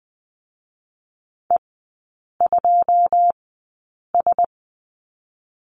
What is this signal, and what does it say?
Morse code, heard as a keyed tone, about 20 wpm, 700 Hz. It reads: E 2 S
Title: 121 Lesson 20e - Number 3 - Three Characters 20wpm